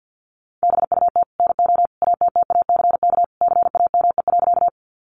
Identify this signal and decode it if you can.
Morse code, heard as a keyed tone, about 48 wpm, 700 Hz. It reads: BUT NO ATTACK CAME.
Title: Sentences from Top 1500 Words 50wpm